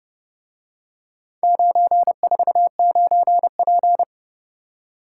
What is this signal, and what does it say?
Morse code, heard as a keyed tone, about 29 wpm, 700 Hz. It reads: 949P